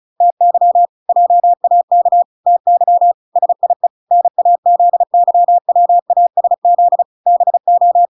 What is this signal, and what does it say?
Morse code, heard as a keyed tone, about 36 wpm, 700 Hz. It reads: TY JAK TY SIE NAZYWASZ BO